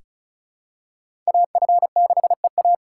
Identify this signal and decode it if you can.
Morse code, heard as a keyed tone, about 34 wpm, 700 Hz. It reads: AF6EA
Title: USA Call Signs - 2x Character Spacing - 35wpm